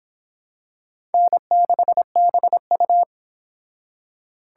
Morse code, heard as a keyed tone, about 25 wpm, 700 Hz. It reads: N6BU